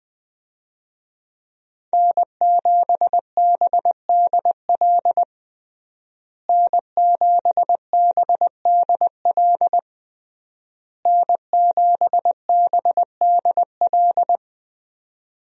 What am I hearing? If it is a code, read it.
Morse code, heard as a keyed tone, about 20 wpm, 700 Hz. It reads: N7BDL N7BDL N7BDL
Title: USA Call Signs - 5 Characters - Repeated 3x - 20wpm